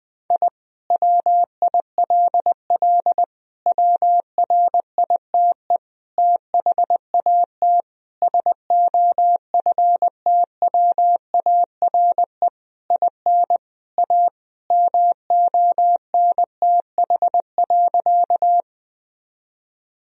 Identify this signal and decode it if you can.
Morse code, heard as a keyed tone, about 20 wpm, 700 Hz. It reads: I WILL WRITE THAT SOFTWARE IN A MONTH.
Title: Sentences from Top 1000 Words - Review Part 3 - 20wpm